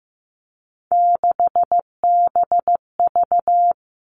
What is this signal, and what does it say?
Morse code, heard as a keyed tone, about 15 wpm, 700 Hz. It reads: 6BV
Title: Three Challenging Letter-Number Combinations 15wpm